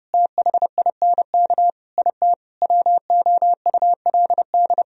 Morse code, heard as a keyed tone, about 36 wpm, 700 Hz. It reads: THINK IT WOULD
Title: Speed-Racing - Sets of 5 Words 30wpm